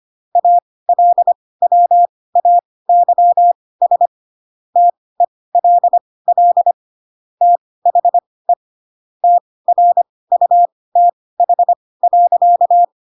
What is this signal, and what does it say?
Morse code, heard as a keyed tone, about 25 wpm, spaced as for 18 wpm, 700 Hz. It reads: ALWAYS TELL THE TRUTH.